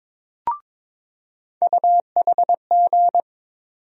Morse code, heard as a keyed tone, about 21 wpm, 700 Hz. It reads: UHG
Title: USA Call Sign Postfixes - 3 Letters 22wpm